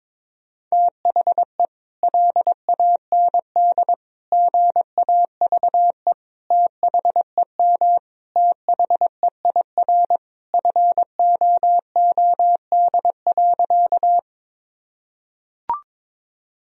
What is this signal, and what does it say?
Morse code, heard as a keyed tone, about 22 wpm, 700 Hz. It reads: THE LAND GAVE THEM THEIR FOOD.
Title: Sentences from Top 1000 Words - Review Part 1 - 22wpm